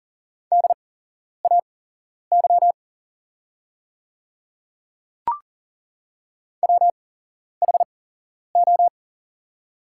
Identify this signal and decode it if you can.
Morse code, heard as a keyed tone, about 40 wpm, spaced as for 11 wpm, 690 Hz. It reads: DAY WHO
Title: 8x Character Spacing - Top 100 Words - 40wpm